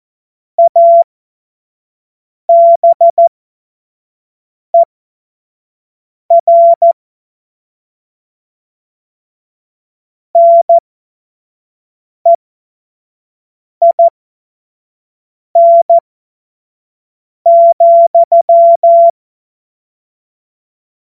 Morse code, heard as a keyed tone, about 14 wpm, spaced as for 5 wpm, 680 Hz. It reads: ABER NEIN,
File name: Morsekurs / Witze / Witz 27. Juni 2023 25 F70